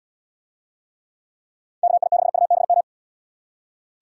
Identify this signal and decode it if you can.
Morse code, heard as a keyed tone, about 70 wpm, 700 Hz. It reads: KI6UBM